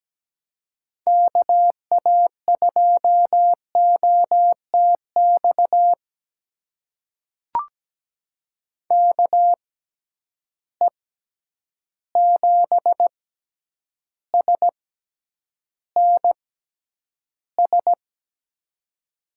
Morse code, heard as a keyed tone, about 17 wpm, spaced as for 9 wpm, 700 Hz. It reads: KA2OTX KE7SNS
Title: USA Call Signs - 6 Characters - 6x Character Spacing - 17wpm